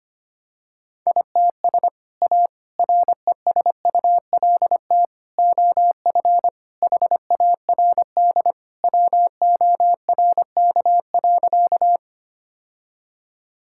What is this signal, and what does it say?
Morse code, heard as a keyed tone, about 25 wpm, 700 Hz. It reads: ITS A RESULT OF HARD WORK.
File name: practice / 25wpm / Sentences with 3-Word Phrases - 25wpm